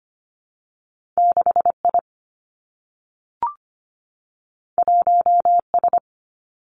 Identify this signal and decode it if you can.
Morse code, heard as a keyed tone, about 25 wpm, 700 Hz. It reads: 6 I 1 S
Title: Two Challenging Letter-Number Combinations 25wpm